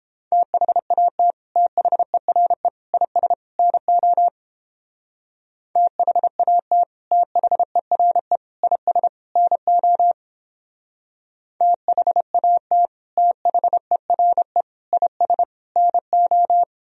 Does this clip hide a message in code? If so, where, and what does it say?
Yes: Morse code from 0.3 to 16.6 s, heard as a keyed tone, about 29 wpm, 700 Hz. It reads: THAT THERE IS NO THAT THERE IS NO THAT THERE IS NO